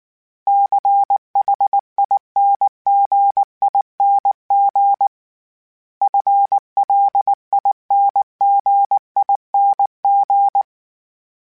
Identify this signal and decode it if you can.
Morse code, heard as a keyed tone, about 19 wpm, 800 Hz. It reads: CHINGING FLINGING
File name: Special INGs-mp3s / Special INGs-mp3s / Dble ings @18